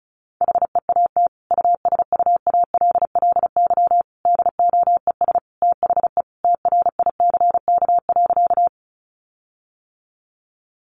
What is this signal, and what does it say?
Morse code, heard as a keyed tone, about 34 wpm, 700 Hz. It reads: HEAT USUALLY DOES THE TRICK.
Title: Sentences from Top 1500 Words 35wpm